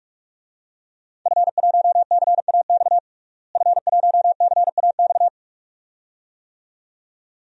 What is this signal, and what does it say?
Morse code, heard as a keyed tone, about 44 wpm, 700 Hz. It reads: F1CAX F1CAX
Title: DXCC-100 Call Signs - 5 Characters - Repeated 2x - 45wpm